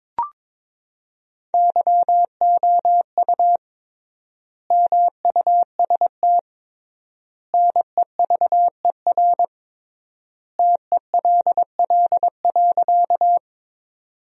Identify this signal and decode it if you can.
Morse code, heard as a keyed tone, about 22 wpm, 700 Hz. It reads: YOU MUST NEVER TELL.
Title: 3x Word-Spacing - Sentences from Top 300 Words - 22wpm